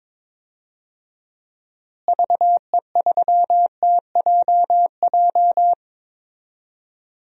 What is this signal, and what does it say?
Morse code, heard as a keyed tone, about 22 wpm, 700 Hz. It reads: VE3TJJ
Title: DXCC-100 Call Signs - 6 Characters - Repeated 3x - 22wpm